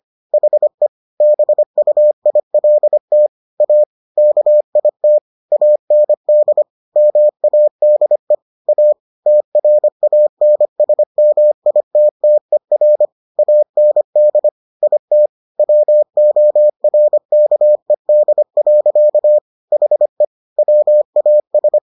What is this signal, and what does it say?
Morse code, heard as a keyed tone, about 25 wpm, 600 Hz. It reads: HE BUILT A KIT AND MADE A TRANSMITTER AND IT WORKED. HE WAS